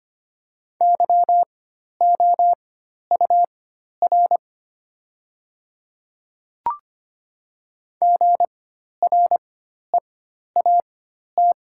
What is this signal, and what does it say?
Morse code, heard as a keyed tone, about 25 wpm, spaced as for 12 wpm, 700 Hz. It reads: YOUR GREAT
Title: 4x Character Spacing - Top 300 Words - 25wpm